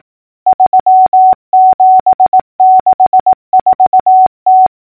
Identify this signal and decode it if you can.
Morse code, heard as a keyed tone, about 18 wpm, 750 Hz. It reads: 3764T
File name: numbers / V2-morse-clicks